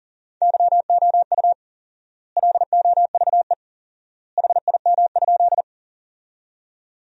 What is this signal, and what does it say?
Morse code, heard as a keyed tone, about 40 wpm, 700 Hz. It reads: YOU LOVE HIM?
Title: 4x Word-Spacing - Sentences from Top 400 Words - 40wpm